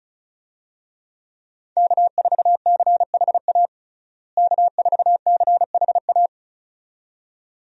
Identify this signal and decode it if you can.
Morse code, heard as a keyed tone, about 34 wpm, 700 Hz. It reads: K4CHA K4CHA
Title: USA Call Signs - 5 Characters - Repeated 2x - 35wpm